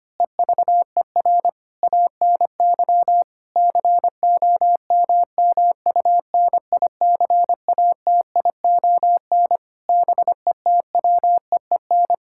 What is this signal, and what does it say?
Morse code, heard as a keyed tone, about 25 wpm, 700 Hz. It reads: EVER ANY COMMUNICATION BETWEEN
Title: Sentences from Top 1000 Words - Review Part 6 - 25wpm